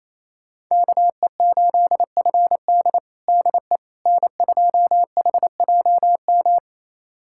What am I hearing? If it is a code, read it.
Morse code, heard as a keyed tone, about 28 wpm, 700 Hz. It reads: KE8FD DE N2HJM
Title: USA Callsign DE Callsign - 5 Characters - 28wpm